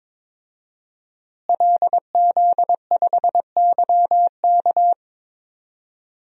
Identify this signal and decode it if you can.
Morse code, heard as a keyed tone, about 22 wpm, 700 Hz. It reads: LZ5YK
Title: DXCC-100 Call Signs - 5 Characters - Repeated 2x - 22wpm